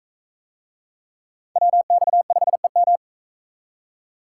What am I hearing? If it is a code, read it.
Morse code, heard as a keyed tone, about 39 wpm, 700 Hz. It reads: WX5EM